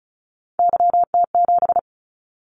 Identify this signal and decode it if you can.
Morse code, heard as a keyed tone, about 36 wpm, 700 Hz. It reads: YT7
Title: Callsign Prefixes for Top 25 DXCC - 35wpm